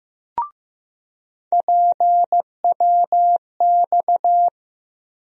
Morse code, heard as a keyed tone, about 15 wpm, 700 Hz. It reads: PWX
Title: USA Call Sign Postfixes - 3 Letters 15wpm